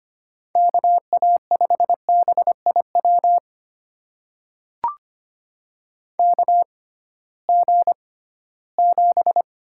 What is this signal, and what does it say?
Morse code, heard as a keyed tone, about 25 wpm, spaced as for 16 wpm, 700 Hz. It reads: KA5BIW KG7
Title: USA Call Signs - 6 Characters - 6x Character Spacing - 25wpm